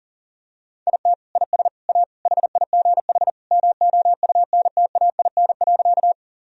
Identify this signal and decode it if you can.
Morse code, heard as a keyed tone, about 40 wpm, 700 Hz. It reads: IT IS A HIGH MOUNTAIN.
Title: Sentences from Top 1000 Words - Review Part 1 - 40wpm